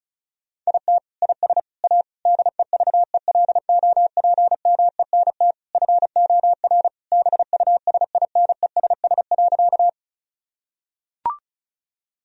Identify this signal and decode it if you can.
Morse code, heard as a keyed tone, about 34 wpm, 700 Hz. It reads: IT IS A DEVELOPMENT FOR BUSINESS.
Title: Sentences from Top 1000 Words - Review Part 4 - 35wpm